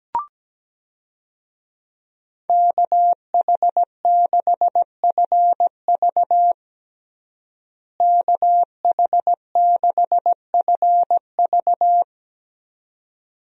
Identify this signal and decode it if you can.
Morse code, heard as a keyed tone, about 17 wpm, 700 Hz. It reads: KH6FV KH6FV
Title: USA Call Signs - 5 Characters - Repeated 2x - 17wpm